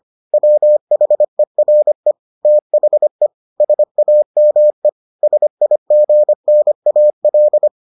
Morse code, heard as a keyed tone, about 25 wpm, 600 Hz. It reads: WHERE THE SAME SIGNAL